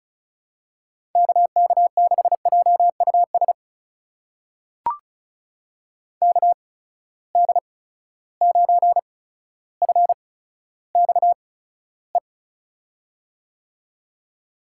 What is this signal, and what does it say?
Morse code, heard as a keyed tone, about 36 wpm, spaced as for 15 wpm, 700 Hz. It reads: KK6JUS KD9FXE
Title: USA Call Signs - 6 Characters - 8x Character Spacing - 35wpm